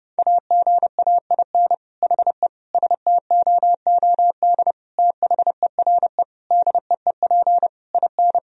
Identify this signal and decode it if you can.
Morse code, heard as a keyed tone, about 29 wpm, 700 Hz. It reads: AGAIN HE STOOD THERE DEEP IN
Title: Sentences from Top 1000 Words - Review Part 3 - 30wpm